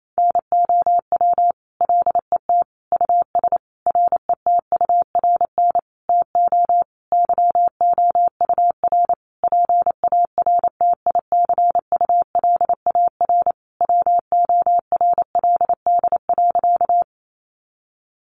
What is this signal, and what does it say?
Morse code, heard as a keyed tone, about 28 wpm, 700 Hz. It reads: NOW LET US RETURN TO YOUR PARTICULAR WORLD.
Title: Sentences from Top 800 Words 28wpm